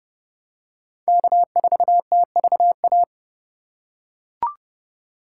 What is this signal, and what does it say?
Morse code, heard as a keyed tone, about 28 wpm, 700 Hz. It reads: K4TVA